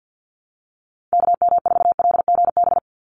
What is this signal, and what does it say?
Morse code, heard as a keyed tone, about 50 wpm, 700 Hz. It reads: KM4LGB
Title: USA Call Signs 50wpm